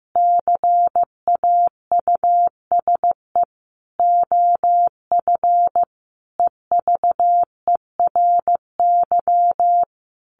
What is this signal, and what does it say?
Morse code, heard as a keyed tone, about 15 wpm, 700 Hz. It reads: CAUSE OF EVERY
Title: Sentences from Top 1400 Words 15wpm